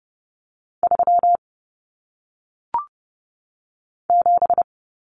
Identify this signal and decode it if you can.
Morse code, heard as a keyed tone, about 30 wpm, 700 Hz. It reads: 3 7
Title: Numbers - 1-Digit 30wpm